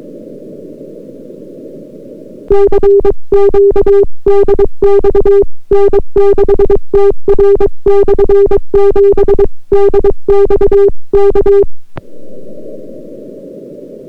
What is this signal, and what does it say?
Morse code, heard as a keyed tone, about 22 wpm, 400 Hz. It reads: CQ DX N6TR/7 DX K